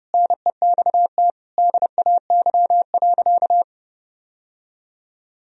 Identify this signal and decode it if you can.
Morse code, heard as a keyed tone, about 30 wpm, 700 Hz. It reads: NEXT DAY.
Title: Sentences from Top 1000 Words - Review Part 1 - 30wpm